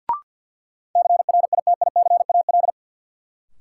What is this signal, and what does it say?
Morse code, heard as a keyed tone, about 50 wpm, 700 Hz. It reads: CRITICAL
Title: Sentences from Top 1100 Words - New Word Drill - 50wpm